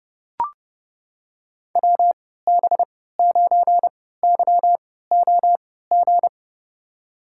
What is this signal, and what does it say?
Morse code, heard as a keyed tone, about 32 wpm, spaced as for 17 wpm, 700 Hz. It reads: WB9YOG
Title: USA Call Signs - 6 Characters - 3x Character Spacing - 30wpm